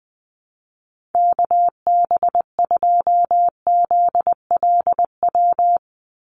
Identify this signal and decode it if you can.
Morse code, heard as a keyed tone, about 20 wpm, 700 Hz. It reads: KB2ZLW
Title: USA Call Signs 20wpm